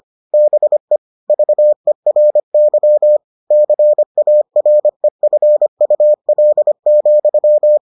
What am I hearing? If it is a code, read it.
Morse code, heard as a keyed tone, about 25 wpm, 600 Hz. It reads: BE VERY CAREFUL,